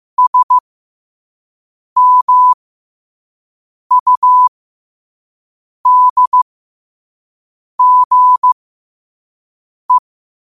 Morse code, heard as a keyed tone, about 15 wpm, spaced as for 5 wpm, 1000 Hz. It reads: SMUDGE
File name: qotdfiles / 5wpm